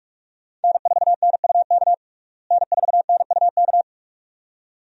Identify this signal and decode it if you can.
Morse code, heard as a keyed tone, about 47 wpm, 700 Hz. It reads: N4NUK N4NUK
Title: USA Call Signs - 5 Characters - Repeated 2x - 45wpm